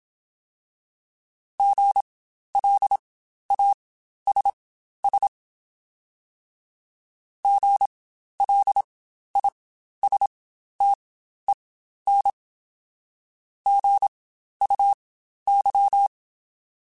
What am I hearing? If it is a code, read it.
Morse code, heard as a keyed tone, about 27 wpm, spaced as for 12 wpm, 800 Hz. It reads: GLASS GLISTEN GUY
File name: Spaced Alphas / Spaced Alphas mp3s / Gs sp @25